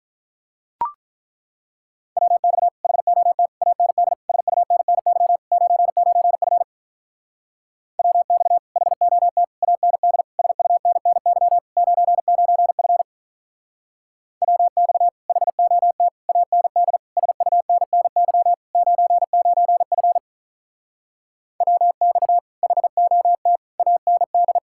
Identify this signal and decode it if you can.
Morse code, heard as a keyed tone, about 42 wpm, 700 Hz. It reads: WX HOT AND SUNNY 99F WX HOT AND SUNNY 99F WX HOT AND SUNNY 99F WX HOT AND
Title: Speed-Racing Common QSO Phrases Encore - 35wpm